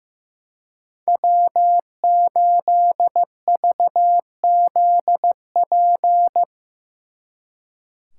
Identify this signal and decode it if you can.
Morse code, heard as a keyed tone, about 17 wpm, 700 Hz. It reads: W8VZP
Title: Speed Racing - USA 1x3 Call Signs 15wpm